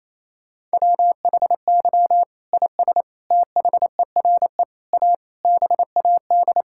Morse code, heard as a keyed tone, about 28 wpm, 700 Hz. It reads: WHY IS THERE A BAD
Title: Sentences from Top 1600 Words 28wpm